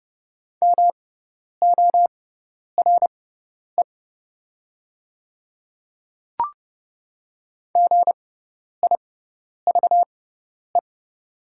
Morse code, heard as a keyed tone, about 30 wpm, spaced as for 10 wpm, 700 Hz. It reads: MORE GIVE
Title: 6x Character Spacing - Top 300 Words - 30wpm